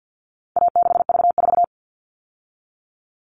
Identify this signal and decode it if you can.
Morse code, heard as a keyed tone, about 50 wpm, 700 Hz. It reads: A6V4
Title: All Cut Numbers - 4-Digit 50wpm